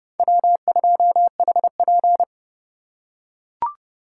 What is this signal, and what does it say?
Morse code, heard as a keyed tone, about 28 wpm, 700 Hz. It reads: W2HP